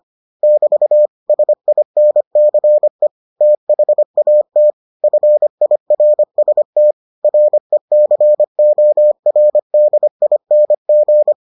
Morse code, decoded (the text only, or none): <BT> SINCE THAT FIRST RECORDING